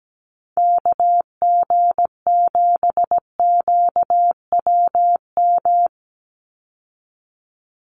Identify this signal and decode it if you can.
Morse code, heard as a keyed tone, about 17 wpm, 700 Hz. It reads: KG7QWM
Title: USA Call Signs 17wpm